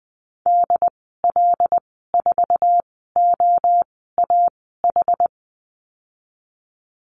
Morse code, heard as a keyed tone, about 20 wpm, spaced as for 14 wpm, 700 Hz. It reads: DL4OAH
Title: 076 Lesson 13g - Letter L - Call Signs 20wpm